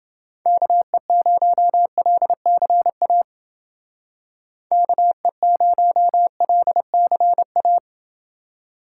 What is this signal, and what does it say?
Morse code, heard as a keyed tone, about 29 wpm, 700 Hz. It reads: KE0LCA KE0LCA